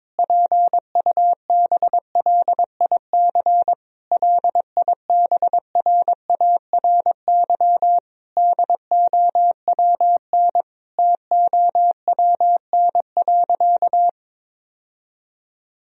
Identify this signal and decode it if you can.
Morse code, heard as a keyed tone, about 22 wpm, 700 Hz. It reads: PUBLIC LIBRARY DOWN TOWN.
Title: Sentences from Top 1000 Words - Review Part 4 - 22wpm